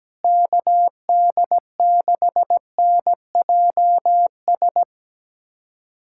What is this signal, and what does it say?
Morse code, heard as a keyed tone, about 17 wpm, 700 Hz. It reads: KD6NJS
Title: USA Callsign DE Callsign 17wpm